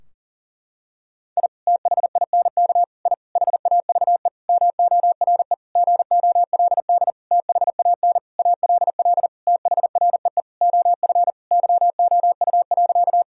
Morse code, heard as a keyed tone, about 37 wpm, 700 Hz. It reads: I THINK I HAVE MORE GOLD THAN ALL THREE OF YOU.
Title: Sentences from Top 1000 Words - Review Part 3 - 40wpm